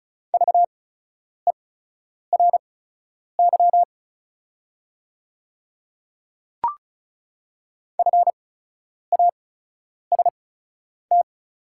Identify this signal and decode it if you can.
Morse code, heard as a keyed tone, about 36 wpm, spaced as for 10 wpm, 700 Hz. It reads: VERY FAST
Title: 8x Character Spacing - Top 500 Words - 35wpm